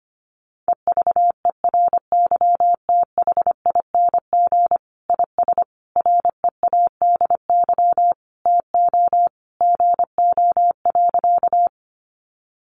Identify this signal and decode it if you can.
Morse code, heard as a keyed tone, about 25 wpm, 700 Hz. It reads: EVERYTHING IS READY TO GO.
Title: Sentences from Top 700 Words 25wpm